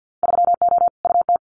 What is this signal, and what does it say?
Morse code, heard as a keyed tone, about 50 wpm, 700 Hz. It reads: 3O UT
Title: Winter Field Day Exchanges 50wpm